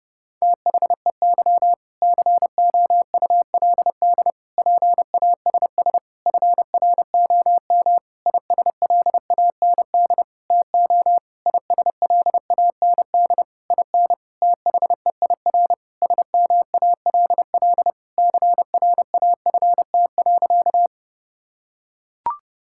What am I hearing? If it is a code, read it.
Morse code, heard as a keyed tone, about 30 wpm, 700 Hz. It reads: THEY COULD PASS FROM ISLAND TO ISLAND IN THEIR SMALL CRAFT.
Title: Sentences from Top 1000 Words - Review Part 6 - 30wpm